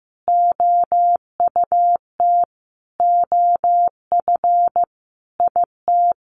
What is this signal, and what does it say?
Morse code, heard as a keyed tone, about 15 wpm, 700 Hz. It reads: OUT OF IT
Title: Sentences from Top 1100 Words 15wpm